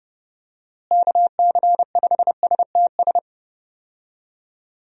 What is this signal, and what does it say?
Morse code, heard as a keyed tone, about 30 wpm, 700 Hz. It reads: KC5STS
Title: USA Call Signs - 6 Characters - Repeated 3x - 30wpm